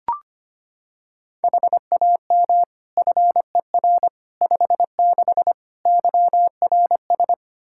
Morse code, heard as a keyed tone, about 25 wpm, 700 Hz. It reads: HAM FER 56 YRS
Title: Practice QSO - Exchange 3 - More Info - 25wpm